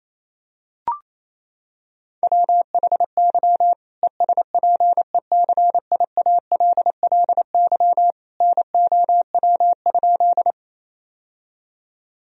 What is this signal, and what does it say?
Morse code, heard as a keyed tone, about 28 wpm, 700 Hz. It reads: WHY ESPECIALLY NOW?
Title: Sentences from Top 1000 Words - Review Part 4 - 28wpm